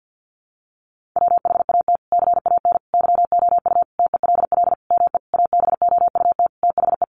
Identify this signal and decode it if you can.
Morse code, heard as a keyed tone, about 50 wpm, 700 Hz. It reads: WHAT CAN YOU TELL ME ABOUT THE
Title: Sentences from Top 1500 Words 50wpm